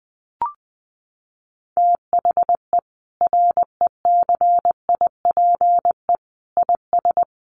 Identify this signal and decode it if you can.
Morse code, heard as a keyed tone, about 20 wpm, 700 Hz. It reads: THE RECIPE IS
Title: Sentences from Top 1200 Words 20wpm